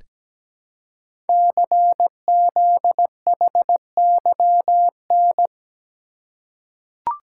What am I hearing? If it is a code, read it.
Morse code, heard as a keyed tone, about 17 wpm, 700 Hz. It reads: CZHYN
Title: Code Groups - 5 Letters - 17wpm